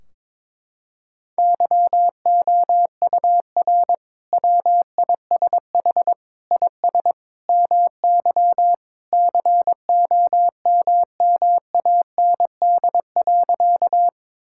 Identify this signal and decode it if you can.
Morse code, heard as a keyed tone, about 22 wpm, 700 Hz. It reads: YOUR WISH IS MY COMMAND.
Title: Sentences from Top 1600 Words 22wpm